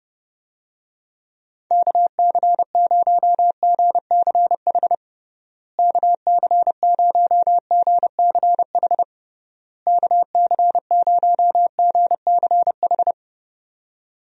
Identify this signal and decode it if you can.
Morse code, heard as a keyed tone, about 32 wpm, 700 Hz. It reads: KC0GCH KC0GCH KC0GCH
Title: USA Call Signs - 6 Characters - Repeated 3x - 30wpm